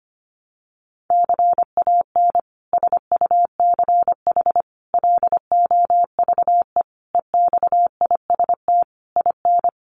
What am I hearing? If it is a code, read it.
Morse code, heard as a keyed tone, about 25 wpm, 700 Hz. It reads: CAN SUCH LOVE EXIST IN
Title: Sentences from Top 1100 Words 25wpm